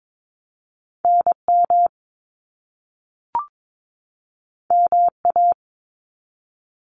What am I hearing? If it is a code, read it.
Morse code, heard as a keyed tone, about 22 wpm, 700 Hz. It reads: NM MA
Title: US States 22wpm